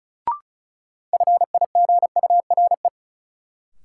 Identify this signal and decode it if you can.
Morse code, heard as a keyed tone, about 36 wpm, 700 Hz. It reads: FIGURE
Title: Sentences from Top 1000 Words - Review Part 2 - New Word Drill - 35wpm